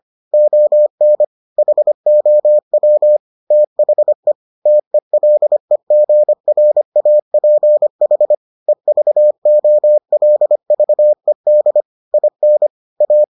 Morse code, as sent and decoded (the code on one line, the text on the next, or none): --- -. / .... --- .-- / - .... / - . .-.. --. .-. .- .--. .... / . ...- --- .-.. ...- . -.. / .. -. / .-
ON HOW THE TELEGRAPH EVOLVED IN A